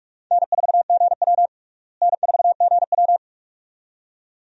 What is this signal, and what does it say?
Morse code, heard as a keyed tone, about 42 wpm, 700 Hz. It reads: N4GW N4GW
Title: USA Call Signs - 4 Characters - Repeated 2x - 45wpm